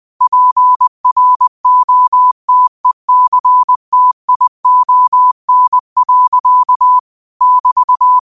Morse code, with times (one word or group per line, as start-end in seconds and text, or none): 0.2-7.0 PROTECTION.
7.4-8.2 <BT>